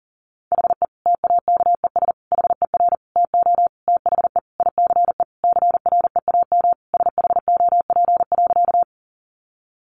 Morse code, heard as a keyed tone, about 40 wpm, 700 Hz. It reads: HE TAKES HER TO THE ICE CREAM SHOP.
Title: Additional Easy Phrases 40wpm